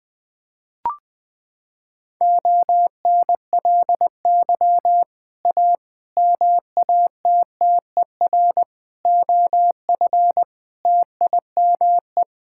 Morse code, heard as a keyed tone, about 20 wpm, 700 Hz. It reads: ONLY A MATTER OF TIME
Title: Sets of 6 Words - 20wpm